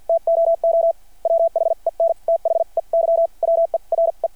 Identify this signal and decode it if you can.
Morse code, heard as a keyed tone, about 52 wpm, spaced as for 41 wpm, 660 Hz. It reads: TOO WHEN THEY WERE
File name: archived / CRSnet